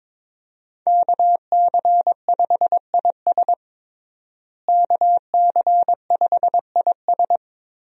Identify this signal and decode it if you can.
Morse code, heard as a keyed tone, about 21 wpm, 700 Hz. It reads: KC5IS KC5IS
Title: USA Call Signs - 5 Characters - Repeated 3x - 22wpm